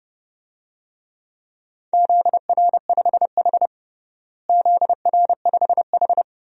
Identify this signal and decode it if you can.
Morse code, heard as a keyed tone, about 30 wpm, 700 Hz. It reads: ZR5H ZR5H
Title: DXCC-100 Call Signs - 4 Characters - Repeated 2x - 30wpm